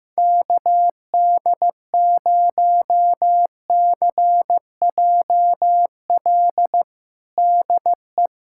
Morse code, heard as a keyed tone, about 15 wpm, 700 Hz. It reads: KD0CJL DE
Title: USA Callsign DE Callsign 15wpm